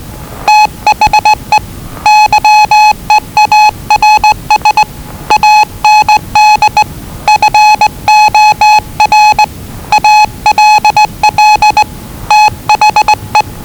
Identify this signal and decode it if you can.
Morse code, heard as a keyed tone, about 20 wpm, 850 Hz. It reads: THE YEARS AND FOR ALL THE
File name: WellingtonRadioZlwCloseDownOnHfAndMf / 01-Zlw 4277khz 417.5 qrt until 0925z